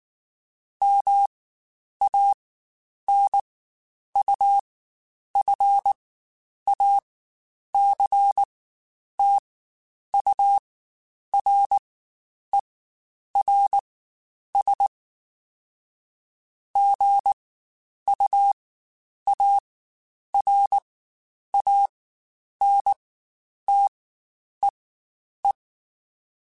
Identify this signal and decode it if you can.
Morse code, heard as a keyed tone, about 19 wpm, spaced as for 9 wpm, 800 Hz. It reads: MANUFACTURERS GUARANTEE
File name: Quotes Spaced / Quotes Spaced-mp3s / Airtight sp @18